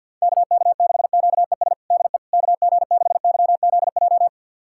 Morse code, heard as a keyed tone, about 51 wpm, 700 Hz. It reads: KK6QES DE KG6YZJ
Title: USA Callsign DE Callsign - 6 Characters - 50wpm